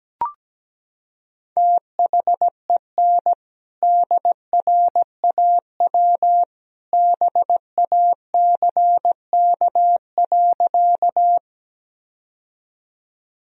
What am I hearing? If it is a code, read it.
Morse code, heard as a keyed tone, about 17 wpm, 700 Hz. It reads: THEN DRAW BACK.
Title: Sentences from Top 1000 Words - Review Part 1 - 17wpm